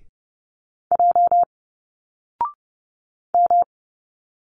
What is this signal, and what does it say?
Morse code, heard as a keyed tone, about 30 wpm, 700 Hz. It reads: J M
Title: Single Letters 30wpm